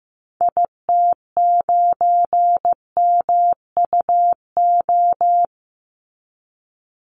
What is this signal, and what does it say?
Morse code, heard as a keyed tone, about 15 wpm, 700 Hz. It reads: IT9MUO
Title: Call Signs Repeated 2x - 15wpm